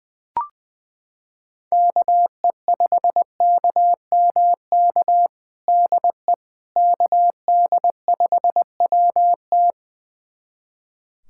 Morse code, heard as a keyed tone, about 20 wpm, 700 Hz. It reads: KE5KMK DE KD5WT
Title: USA Callsign DE Callsign 20wpm